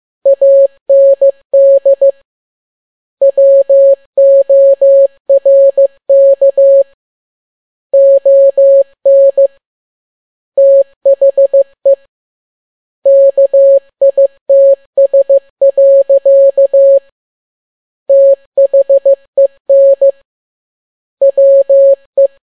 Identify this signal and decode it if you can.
Morse code, heard as a keyed tone, about 15 wpm, 550 Hz. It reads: AND WORK ON THE KITS. THEN WE
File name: n6tt p1 15C13